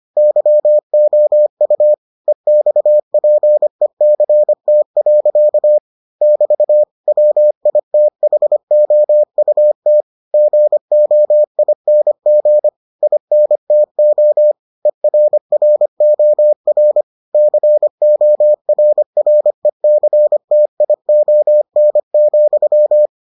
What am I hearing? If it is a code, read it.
Morse code, heard as a keyed tone, about 25 wpm, 600 Hz. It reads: YOU EXPECT. <BT> WITHOUT GOING INTO ERROR CORRECTION,